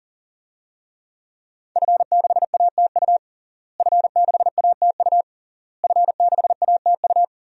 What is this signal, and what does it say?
Morse code, heard as a keyed tone, about 37 wpm, 700 Hz. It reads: F6ATU F6ATU F6ATU
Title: DXCC-100 Call Signs - 5 Characters - Repeated 3x - 40wpm